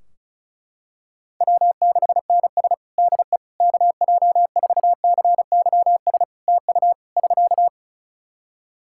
Morse code, heard as a keyed tone, about 34 wpm, 700 Hz. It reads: W6NS DE KJ4CYS TU <SK>
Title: Practice QSO - Exchange 4 - Ending - 35wpm